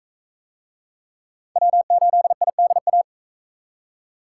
Morse code, heard as a keyed tone, about 42 wpm, 700 Hz. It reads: W8IDA